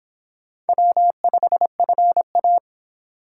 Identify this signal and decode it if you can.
Morse code, heard as a keyed tone, about 25 wpm, 700 Hz. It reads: W5FA